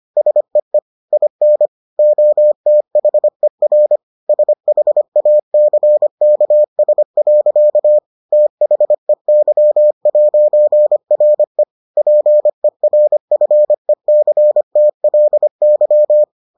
Morse code, heard as a keyed tone, about 25 wpm, 600 Hz. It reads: SEE IN OTHER SHACKS. THEY'RE PERFECTLY